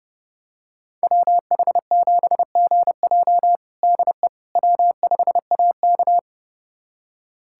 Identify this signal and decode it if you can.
Morse code, heard as a keyed tone, about 30 wpm, 700 Hz. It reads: WH7GJ DE W5AK
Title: USA Callsign DE Callsign 30wpm